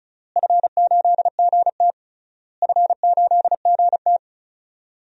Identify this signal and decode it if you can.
Morse code, heard as a keyed tone, about 34 wpm, 700 Hz. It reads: F8GT F8GT
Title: DXCC-100 Call Signs - 4 Characters - Repeated 3x - 35wpm